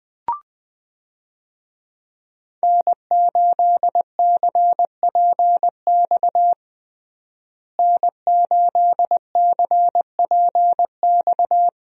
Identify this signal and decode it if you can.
Morse code, heard as a keyed tone, about 20 wpm, 700 Hz. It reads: N8CPX N8CPX
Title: USA Call Signs - 5 Characters - Repeated 3x - 20wpm